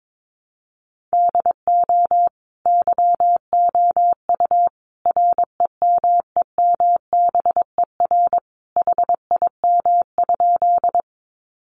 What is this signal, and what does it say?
Morse code, heard as a keyed tone, about 22 wpm, 700 Hz. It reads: DO YOU REMEMBER HIM?
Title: Sentences from Top 400 Words 22wpm